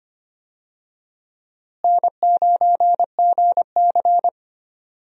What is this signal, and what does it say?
Morse code, heard as a keyed tone, about 25 wpm, 700 Hz. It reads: N9GC
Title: USA Call Signs - 4 Characters - Repeated 3x - 25wpm